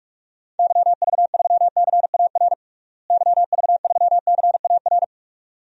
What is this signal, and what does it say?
Morse code, heard as a keyed tone, about 42 wpm, 700 Hz. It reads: YV3CAR YV3CAR
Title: DXCC-100 Call Signs - 6 Characters - Repeated 2x - 45wpm